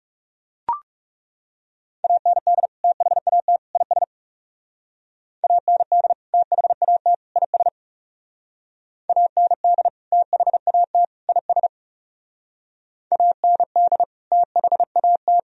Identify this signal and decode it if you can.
Morse code, heard as a keyed tone, about 36 wpm, 700 Hz. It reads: AND THAT IS AND THAT IS AND THAT IS AND THAT
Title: Speed-Racing - Sets of 3 Words 30wpm